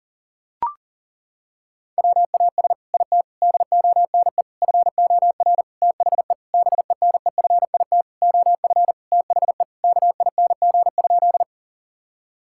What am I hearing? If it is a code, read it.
Morse code, heard as a keyed tone, about 40 wpm, 700 Hz. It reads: WAS IT DONE FOR THE BENEFIT OF THE KING?
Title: Sentences from Top 1000 Words - Review Part 6 - 40wpm